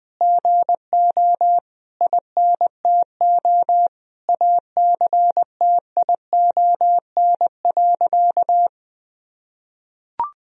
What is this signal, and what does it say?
Morse code, heard as a keyed tone, about 20 wpm, 700 Hz. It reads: GO INTO ACTION.
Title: Sentences from Top 1000 Words - Review Part 4 - 20wpm